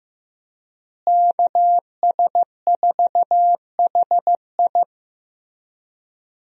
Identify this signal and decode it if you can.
Morse code, heard as a keyed tone, about 15 wpm, 700 Hz. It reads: KS4HI
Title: USA Call Signs - 5 Characters - Repeated 3x - 15wpm